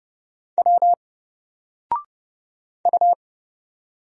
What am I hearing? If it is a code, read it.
Morse code, heard as a keyed tone, about 30 wpm, 700 Hz. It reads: W U
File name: Single-Letter-Number-30wpm